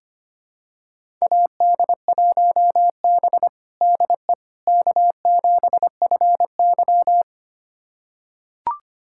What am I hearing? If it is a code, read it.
Morse code, heard as a keyed tone, about 25 wpm, 700 Hz. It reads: AD1B DE K7FY
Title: USA Callsign DE Callsign - 4 Characters - 25wpm